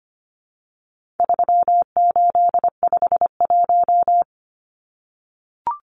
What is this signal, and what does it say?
Morse code, heard as a keyed tone, about 25 wpm, 700 Hz. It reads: 3851
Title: Numbers - 4-Digit 25wpm